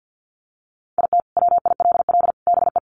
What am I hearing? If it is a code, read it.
Morse code, heard as a keyed tone, about 50 wpm, 700 Hz. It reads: IT WILL BE
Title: Sets of 3 Words 50wpm